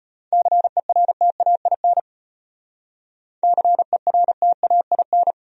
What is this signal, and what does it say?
Morse code, heard as a keyed tone, about 32 wpm, 700 Hz. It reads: CERTAIN CERTAIN